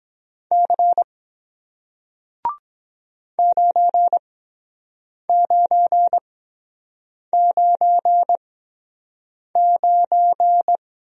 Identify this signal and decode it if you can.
Morse code, heard as a keyed tone, about 23 wpm, spaced as for 7 wpm, 700 Hz. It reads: C 9999